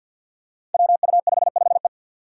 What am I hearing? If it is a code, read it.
Morse code, heard as a keyed tone, about 47 wpm, 700 Hz. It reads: WU55E
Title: Five Challenging Letter-Number Combinations 50wpm